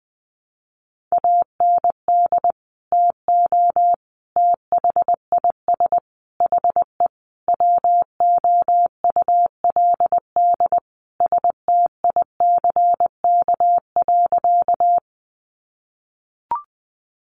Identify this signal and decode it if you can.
Morse code, heard as a keyed tone, about 20 wpm, 700 Hz. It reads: AND TO THIS HE WOULD STICK.
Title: Sentences from Top 1200 Words 20wpm